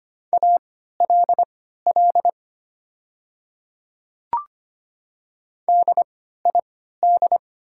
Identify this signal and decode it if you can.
Morse code, heard as a keyed tone, about 25 wpm, spaced as for 14 wpm, 700 Hz. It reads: ALL DID
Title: 3x Character Spacing - Top 100 Words - 25wpm